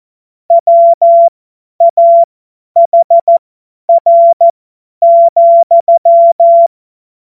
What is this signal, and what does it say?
Morse code, heard as a keyed tone, about 14 wpm, spaced as for 10 wpm, 680 Hz. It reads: WAHR,
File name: Morsekurs / Witze / Witz 04. Oktober 2022 50 F70